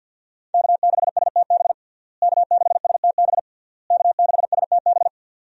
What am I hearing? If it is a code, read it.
Morse code, heard as a keyed tone, about 46 wpm, 700 Hz. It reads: K6STB K6STB K6STB
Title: USA Call Signs - 5 Characters - Repeated 3x - 50wpm